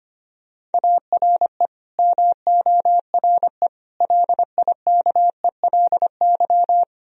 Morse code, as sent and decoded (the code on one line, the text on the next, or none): .- .-. / -- --- .-. / .-.. .. -.- . .-.. -.--
ARE MORE LIKELY